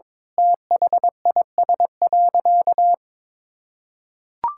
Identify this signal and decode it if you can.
Morse code, heard as a keyed tone, about 22 wpm, 700 Hz. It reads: THIS.
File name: practice / 22wpm / Sentences with 8-Word Phrases - 22wpm